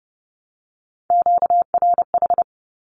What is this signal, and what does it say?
Morse code, heard as a keyed tone, about 30 wpm, 700 Hz. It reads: QRH
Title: Q-Codes with Spoken Meaning - 30wpm